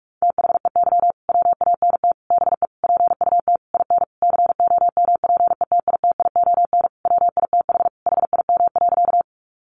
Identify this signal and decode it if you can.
Morse code, heard as a keyed tone, about 44 wpm, 700 Hz. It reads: THEY WANT BE PUT IN COMPETITION WITH HIM.
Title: Sentences from Top 1000 Words 45wpm